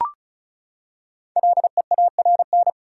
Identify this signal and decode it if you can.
Morse code, heard as a keyed tone, about 34 wpm, 700 Hz. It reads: LEARN
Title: Five Letter Words - 35wpm